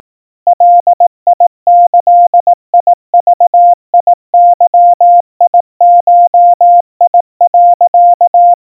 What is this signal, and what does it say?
Morse code (Japanese Wabun code), heard as a keyed tone, about 18 wpm, 700 Hz. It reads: ガギグゲゴ、
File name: iroha 60 cpm 700 Hz Crispy